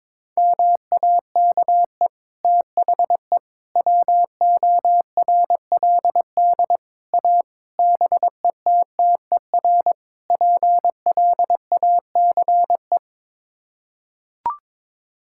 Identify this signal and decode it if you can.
Morse code, heard as a keyed tone, about 22 wpm, 700 Hz. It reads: MAKE THE WORLD A BETTER PLACE
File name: practice / 22wpm / Sets of 6 Words - 22wpm